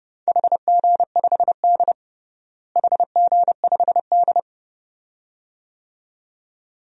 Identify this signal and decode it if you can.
Morse code, heard as a keyed tone, about 30 wpm, 700 Hz. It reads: HG5D HG5D
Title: DXCC-100 Call Signs - 4 Characters - Repeated 2x - 30wpm